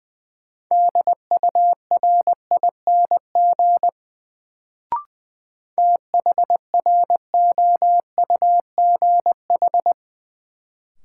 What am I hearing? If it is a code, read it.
Morse code, heard as a keyed tone, about 20 wpm, 700 Hz. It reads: DURING THROUGH
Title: Sets of 2 Words - New Word Drill - 20wpm